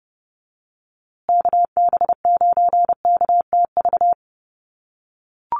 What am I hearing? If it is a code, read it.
Morse code, heard as a keyed tone, about 30 wpm, 700 Hz. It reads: KB9KTV
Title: USA Call Signs Repeated 2x 30wpm